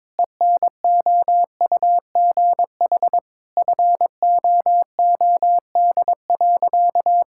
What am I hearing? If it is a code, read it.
Morse code, heard as a keyed tone, about 22 wpm, 700 Hz. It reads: ENOUGH FOOD.
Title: Sentences from Top 1000 Words - Review Part 5 - 22wpm